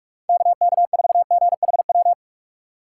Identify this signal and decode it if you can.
Morse code, heard as a keyed tone, about 46 wpm, 700 Hz. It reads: KK4GHW
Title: USA Call Signs - 6 Characters - 2x Character Spacing - 45wpm